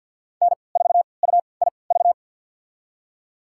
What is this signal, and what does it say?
Morse code, heard as a keyed tone, about 53 wpm, spaced as for 29 wpm, 700 Hz. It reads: N4UIV
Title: USA Call Signs - 3x Character Spacing - 50wpm